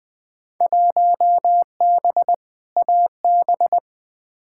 Morse code, heard as a keyed tone, about 20 wpm, 700 Hz. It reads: 1B AB
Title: Field Day Exchanges - 20wpm